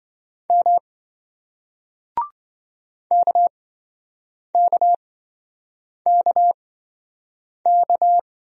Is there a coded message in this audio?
Yes: Morse code from 0.5 to 8.2 s, heard as a keyed tone, about 27 wpm, spaced as for 7 wpm, 700 Hz. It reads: M KKKK